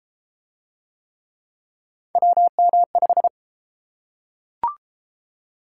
Morse code, heard as a keyed tone, about 32 wpm, 700 Hz. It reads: WM5